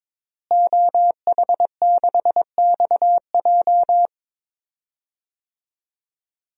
Morse code, heard as a keyed tone, about 22 wpm, 700 Hz. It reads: OH6XJ
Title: DXCC-100 Call Signs - 5 Characters - Repeated 2x - 22wpm